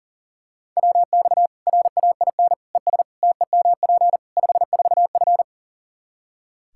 Morse code, heard as a keyed tone, about 40 wpm, 700 Hz. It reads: WX RAIN ES TEMP 54F
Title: Practice QSO - Exchange 2 - Info - 40wpm